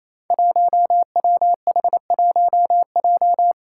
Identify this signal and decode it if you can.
Morse code, heard as a keyed tone, about 28 wpm, 700 Hz. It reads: 1WH1J
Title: Five Challenging Letter-Number Combinations 28wpm